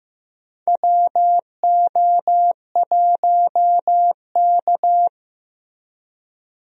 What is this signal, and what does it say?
Morse code, heard as a keyed tone, about 15 wpm, 700 Hz. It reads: WO1K
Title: USA Call Signs - 4 Characters - Repeated 2x - 15wpm